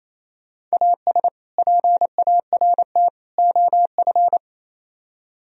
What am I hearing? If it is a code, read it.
Morse code, heard as a keyed tone, about 34 wpm, 700 Hz. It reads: AS PART OF
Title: Speed-Racing - Sets of 3 Words 28wpm